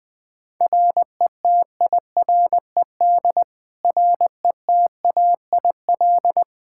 Morse code, heard as a keyed tone, about 21 wpm, 700 Hz. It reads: RETIRED RETAIL
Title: Practice QSO - Exchange 3 - More Info - 20wpm